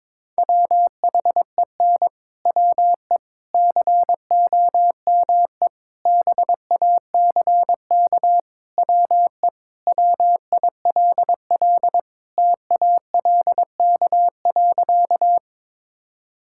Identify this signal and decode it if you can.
Morse code, heard as a keyed tone, about 22 wpm, 700 Hz. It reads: WHEN WE COME BACK WE WILL TALK.
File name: practice / 22wpm / Sentences with 4-Word Phrases - 22wpm